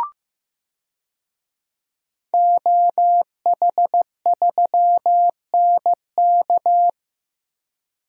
Morse code, heard as a keyed tone, about 15 wpm, 700 Hz. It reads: OH3NK
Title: DXCC-100 Call Signs - 5 Characters - 15wpm